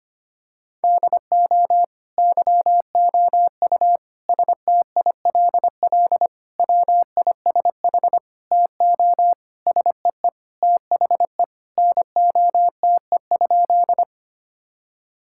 Morse code, heard as a keyed tone, about 25 wpm, 700 Hz. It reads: DO YOU STILL WISH TO SEE THE NOTE?
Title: Sentences from Top 1600 Words 25wpm